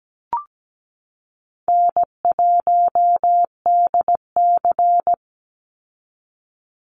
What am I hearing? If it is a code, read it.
Morse code, heard as a keyed tone, about 17 wpm, 700 Hz. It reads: N1DC
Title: Call Signs - 17wpm